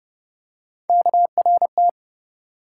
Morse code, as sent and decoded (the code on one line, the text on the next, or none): -.- .-. -
KRT